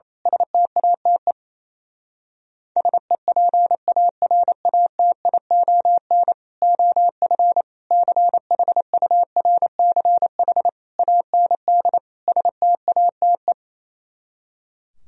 Morse code, heard as a keyed tone, about 27 wpm, 700 Hz. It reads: STATE SEPARATION OF CHURCH AND STATE
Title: Speed-Racing - Sets of 5 Words 28wpm